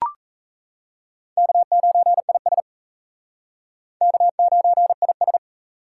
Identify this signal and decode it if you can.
Morse code, heard as a keyed tone, about 35 wpm, 700 Hz. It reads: K9IS K9IS
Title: Speed Racing - USA 1x2 Call Signs 28wpm